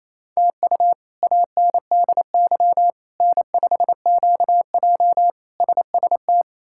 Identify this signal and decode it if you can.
Morse code, heard as a keyed tone, about 27 wpm, 700 Hz. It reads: TU ANDY N5QJ SST
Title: K1USN SST QSO Practice - 28wpm